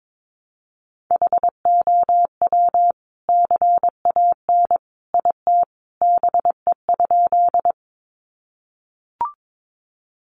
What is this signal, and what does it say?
Morse code, heard as a keyed tone, about 22 wpm, 700 Hz. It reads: HOW CAN IT BE?
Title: Sentences from Top 100 Words 22wpm